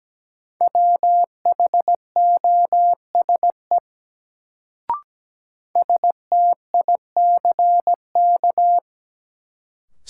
Morse code, heard as a keyed tone, about 17 wpm, 700 Hz. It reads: WHOSE STICK
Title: Sentences from Top 1200 Words - New Word Drill - 17wpm